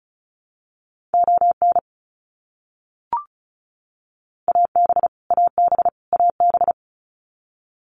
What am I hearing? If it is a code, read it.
Morse code, heard as a keyed tone, about 34 wpm, 700 Hz. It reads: ON AB AB AB
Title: Canadian Provinces and Territories - Repeated 3x - 35wpm